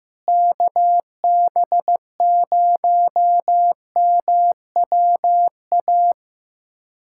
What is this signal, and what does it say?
Morse code, heard as a keyed tone, about 15 wpm, 700 Hz. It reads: KB0MWA
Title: USA Callsign DE Callsign - 6 Characters - 15wpm